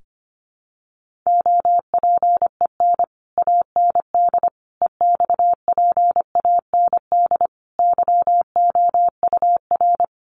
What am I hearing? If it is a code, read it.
Morse code, heard as a keyed tone, about 25 wpm, 700 Hz. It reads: OPEN AND EXPAND YOUR
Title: Sentences from Top 1400 Words 25wpm